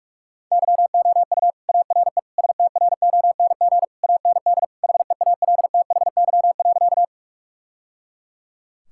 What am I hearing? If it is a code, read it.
Morse code, heard as a keyed tone, about 41 wpm, 700 Hz. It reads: YOU ARE STRONG AND HEALTHY.
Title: Sentences from Top 1000 Words - Review Part 5 - 45wpm